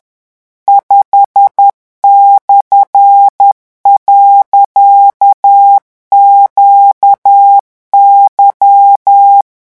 Morse code, heard as a keyed tone, about 10 wpm, 800 Hz. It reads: 5/.QY
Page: REVISION 18 - 10 mots minute